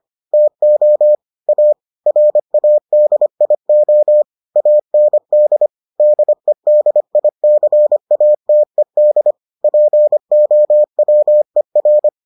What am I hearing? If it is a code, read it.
Morse code, heard as a keyed tone, about 25 wpm, 600 Hz. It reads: TO A RADIO AND DEDICATED POWER